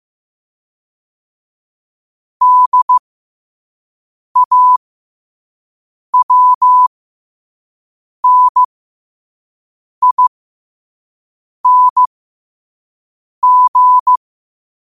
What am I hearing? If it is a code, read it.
Morse code, heard as a keyed tone, about 15 wpm, spaced as for 5 wpm, 1000 Hz. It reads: DAWNING
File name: qotdfiles / 5wpm